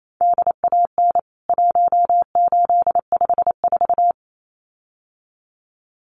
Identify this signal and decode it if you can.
Morse code, heard as a keyed tone, about 28 wpm, 700 Hz. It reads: DAN 1854
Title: CWT Exchange 28wpm